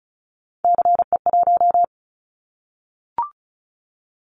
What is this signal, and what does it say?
Morse code, heard as a keyed tone, about 36 wpm, 700 Hz. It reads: CE1
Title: Callsign Prefixes for Top 100 DXCC - 35wpm